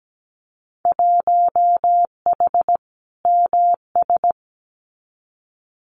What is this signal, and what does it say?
Morse code, heard as a keyed tone, about 17 wpm, 700 Hz. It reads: 1H MS
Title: Winter Field Day Exchanges 17wpm